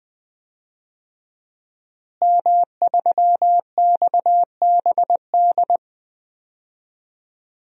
Morse code, heard as a keyed tone, about 20 wpm, 700 Hz. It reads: M3XBD
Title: DXCC-100 Call Signs - 5 Characters - Repeated 3x - 20wpm